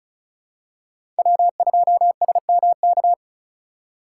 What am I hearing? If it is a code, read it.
Morse code, heard as a keyed tone, about 34 wpm, 700 Hz. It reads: W2SMK
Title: USA Call Signs - 5 Characters - 3x Character Spacing - 35wpm